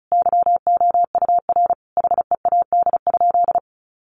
Morse code, heard as a keyed tone, about 34 wpm, 700 Hz. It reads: YOUR HEAD?
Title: Sentences from Top 600 Words 35wpm